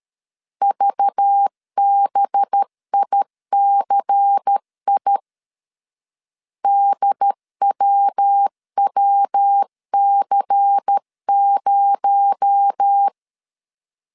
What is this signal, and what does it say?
Morse code, heard as a keyed tone, about 13 wpm, 800 Hz. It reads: VBICI DWWC0
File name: CW / DET / 1 6 1500C S C 12wpm 3533